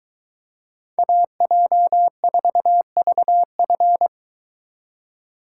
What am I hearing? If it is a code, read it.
Morse code, heard as a keyed tone, about 23 wpm, 700 Hz. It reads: AJ4VF